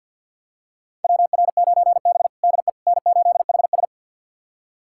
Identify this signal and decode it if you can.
Morse code, heard as a keyed tone, about 52 wpm, 700 Hz. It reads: WR9B DE N8HS
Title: USA Callsign DE Callsign - 4 Characters - 50wpm